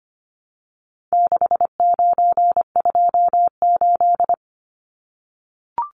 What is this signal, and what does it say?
Morse code, heard as a keyed tone, about 25 wpm, 700 Hz. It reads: 6928
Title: Numbers - 4-Digit 25wpm